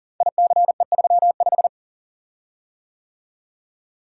Morse code, heard as a keyed tone, about 39 wpm, 700 Hz. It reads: ICE35
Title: Code Groups - 5 Letter:Numbers - 40wpm